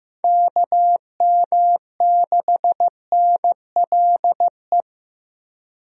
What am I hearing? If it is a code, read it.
Morse code, heard as a keyed tone, about 15 wpm, 700 Hz. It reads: KM6NLE
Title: USA Call Signs - 6 Characters - Repeated 3x - 15wpm